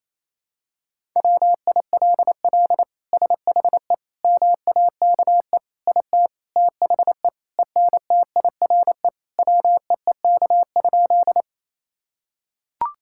Morse code, heard as a keyed tone, about 28 wpm, 700 Hz. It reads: WILL SHE MAKE IT THE ENTIRE WEEK?
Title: Sentences from Top 1000 Words - Review Part 5 - 28wpm